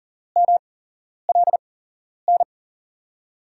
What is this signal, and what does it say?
Morse code, heard as a keyed tone, about 40 wpm, spaced as for 11 wpm, 700 Hz. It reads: MLN
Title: USA Call Signs - 8x Character Spacing - 40wpm